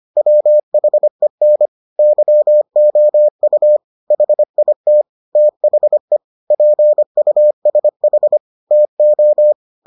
Morse code, heard as a keyed tone, about 25 wpm, 600 Hz. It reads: WHEN YOU HIT THE PUSH TO